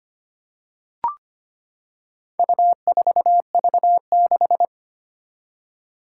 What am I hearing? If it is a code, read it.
Morse code, heard as a keyed tone, about 25 wpm, 700 Hz. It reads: U4V6
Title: Four Challenging Letter-Number Combinations 25wpm